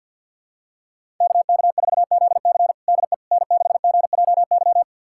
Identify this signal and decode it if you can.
Morse code, heard as a keyed tone, about 45 wpm, 690 Hz. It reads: KK4ZC DE N6GJY
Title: USA Callsign DE Callsign - 5 Characters - 50wpm